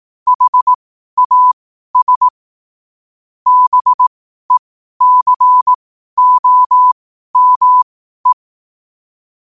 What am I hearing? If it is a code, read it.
Morse code, heard as a keyed tone, about 18 wpm, spaced as for 12 wpm, 1000 Hz. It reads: HAS BECOME
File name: qotdfiles / 12wpm